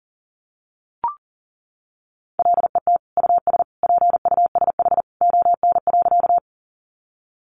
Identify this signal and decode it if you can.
Morse code, heard as a keyed tone, about 40 wpm, 700 Hz. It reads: LET US PUSH ON.
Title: Sentences from Top 1300 Words 40wpm